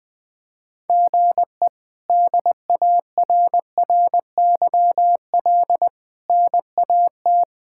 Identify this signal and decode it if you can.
Morse code, heard as a keyed tone, about 19 wpm, 700 Hz. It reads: GE DARRYL NAT
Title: K1USN SST QSO Practice - 20wpm